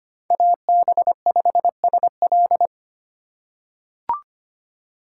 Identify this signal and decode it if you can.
Morse code, heard as a keyed tone, about 25 wpm, 700 Hz. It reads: AB5SL
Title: USA Call Signs - 5 Characters - 3x Character Spacing - 25wpm